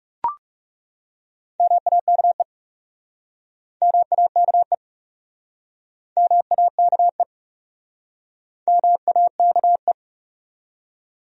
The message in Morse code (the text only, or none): MAKE MAKE MAKE MAKE